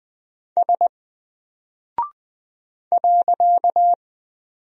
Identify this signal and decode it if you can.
Morse code, heard as a keyed tone, about 20 wpm, 700 Hz. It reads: S .
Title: Single Character (Letter:Number:Punctuation:Prosign) 20wpm